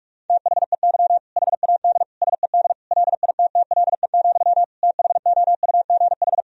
Morse code, heard as a keyed tone, about 41 wpm, 700 Hz. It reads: THEY HAD SED LITTLE, THOUGH
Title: Sentences from Top 400 Words and Abbreviations 45wpm